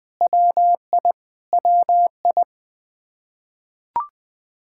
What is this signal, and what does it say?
Morse code, heard as a keyed tone, about 21 wpm, 700 Hz. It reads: WI WI
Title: K1USN SST QSO Practice 20wpm